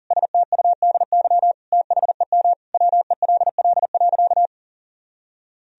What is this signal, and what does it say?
Morse code, heard as a keyed tone, about 40 wpm, 700 Hz. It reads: STUDY THEM WELL.
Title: Sentences from Top 300 Words and Abbreviations 40wpm